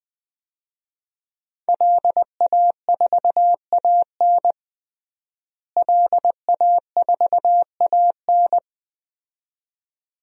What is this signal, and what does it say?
Morse code, heard as a keyed tone, about 20 wpm, 700 Hz. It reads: LA4AN LA4AN
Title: DXCC-100 Call Signs - 5 Characters - Repeated 2x - 20wpm